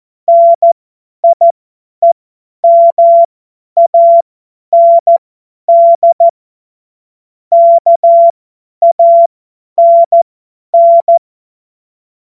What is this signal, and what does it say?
Morse code, heard as a keyed tone, about 14 wpm, spaced as for 10 wpm, 680 Hz. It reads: NIEMAND KANN
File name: Morsekurs / Witze / Witz 24 Juni 2025 50 F70